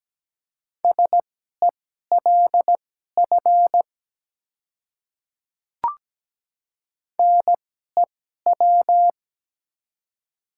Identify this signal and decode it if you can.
Morse code, heard as a keyed tone, about 17 wpm, spaced as for 12 wpm, 700 Hz. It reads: SELF NEW
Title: 2x Character Spacing - Top 300 Words - 17wpm